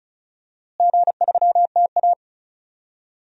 Morse code, heard as a keyed tone, about 34 wpm, 700 Hz. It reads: G3TA
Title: DXCC-100 Call Signs - 4 Characters - Repeated 3x - 35wpm